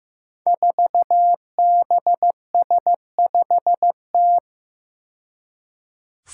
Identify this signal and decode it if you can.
Morse code, heard as a keyed tone, about 15 wpm, 700 Hz. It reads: 4BS5T
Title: Five Challenging Letter-Number Combinations 15wpm